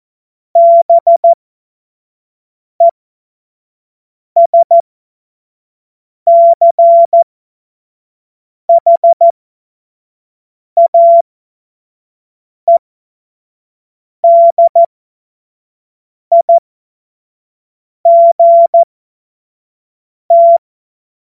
Morse code, heard as a keyed tone, about 14 wpm, spaced as for 5 wpm, 680 Hz. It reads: BESCHAEDIGT
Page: cw-witze